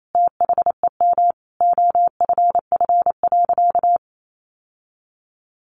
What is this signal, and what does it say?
Morse code, heard as a keyed tone, about 28 wpm, 700 Hz. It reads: THEM OFF.
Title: Sentences from Top 1000 Words 28wpm